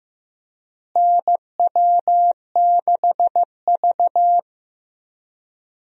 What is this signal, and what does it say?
Morse code, heard as a keyed tone, about 16 wpm, 700 Hz. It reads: NW6V
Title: USA Call Signs - 4 Characters - Repeated 3x - 15wpm